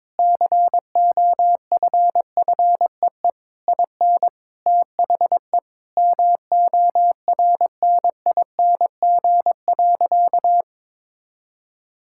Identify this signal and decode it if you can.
Morse code, heard as a keyed tone, about 22 wpm, 700 Hz. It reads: COFFEE IN THE MORNING.
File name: Difficult Sentences with Hints - 22wpm